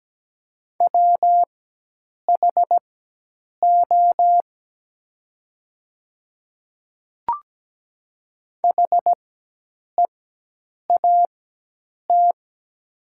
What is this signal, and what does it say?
Morse code, heard as a keyed tone, about 17 wpm, spaced as for 8 wpm, 700 Hz. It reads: WHO HEAT
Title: 4x Character Spacing - Top 500 Words - 17wpm